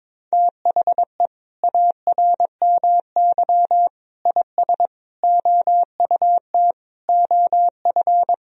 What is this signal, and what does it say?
Morse code, heard as a keyed tone, about 21 wpm, 700 Hz. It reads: THE ARMY IS OUT OF
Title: Sentences from Top 1000 Words - Review Part 4 - 22wpm